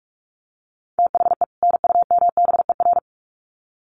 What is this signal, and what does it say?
Morse code, heard as a keyed tone, about 44 wpm, 700 Hz. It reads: THE NUMBER
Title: Sets of 2 Words 45wpm